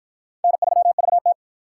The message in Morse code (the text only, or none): N3VT